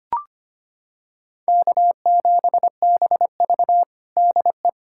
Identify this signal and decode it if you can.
Morse code, heard as a keyed tone, about 25 wpm, 700 Hz. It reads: K7BV DE
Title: Practice QSO - Exchange 4 - Ending - 25wpm